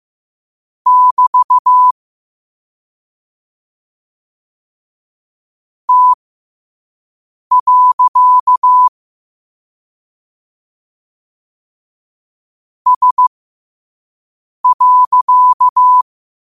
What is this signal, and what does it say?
Morse code, heard as a keyed tone, about 15 wpm, spaced as for 5 wpm, 1000 Hz. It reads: <BT> T. S.
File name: qotdfiles / 5wpm